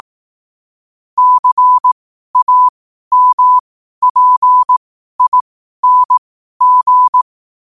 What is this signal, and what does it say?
Morse code, heard as a keyed tone, about 18 wpm, spaced as for 12 wpm, 1000 Hz. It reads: CAMPING
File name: qotdfiles / 12wpm